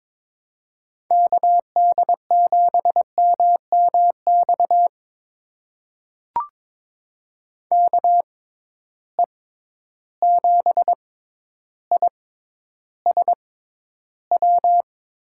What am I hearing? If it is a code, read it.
Morse code, heard as a keyed tone, about 22 wpm, spaced as for 11 wpm, 700 Hz. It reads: KD7MMX KE7ISW
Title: USA Call Signs - 6 Characters - 6x Character Spacing - 22wpm